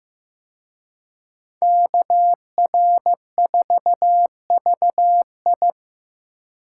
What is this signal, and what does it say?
Morse code, heard as a keyed tone, about 18 wpm, 700 Hz. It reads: KR4VI
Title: Speed Racing - USA Call Signs 15wpm